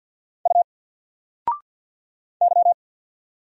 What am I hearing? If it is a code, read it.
Morse code, heard as a keyed tone, about 48 wpm, 700 Hz. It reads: U Y
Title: Single Character (Letter:Number:Punctuation:Prosign) 50wpm